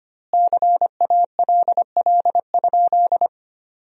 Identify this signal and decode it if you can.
Morse code, heard as a keyed tone, about 25 wpm, 700 Hz. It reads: CALL?
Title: 2x Word-Spacing - Sentences from Top 100 Words - 25wpm